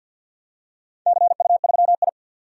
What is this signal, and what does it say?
Morse code, heard as a keyed tone, about 46 wpm, 700 Hz. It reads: CU3I
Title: DXCC-100 Call Signs - 4 Characters - 50wpm